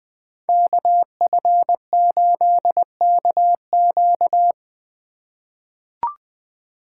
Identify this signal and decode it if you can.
Morse code, heard as a keyed tone, about 20 wpm, 700 Hz. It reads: KF8KQ
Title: USA Call Signs - 5 Characters - 4x Character Spacing - 20wpm